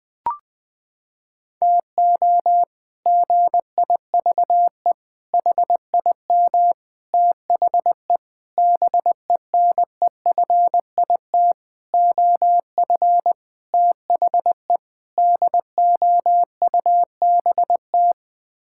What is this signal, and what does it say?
Morse code, heard as a keyed tone, about 21 wpm, 700 Hz. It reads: TO GIVE HIM THE BENEFIT OF THE DOUBT
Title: Sets of 8 Words - 20wpm